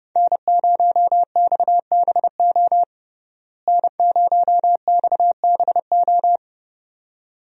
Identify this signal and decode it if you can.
Morse code, heard as a keyed tone, about 30 wpm, 700 Hz. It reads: N0XBO N0XBO
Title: USA Call Signs - 5 Characters - Repeated 2x - 30wpm